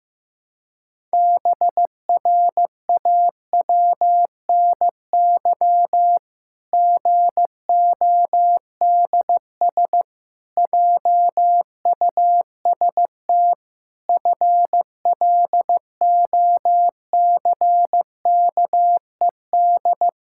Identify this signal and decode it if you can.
Morse code, heard as a keyed tone, about 15 wpm, 700 Hz. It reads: BRAWNY GODS JUST FLOCKED
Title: Pangram Challenge v2 - 15wpm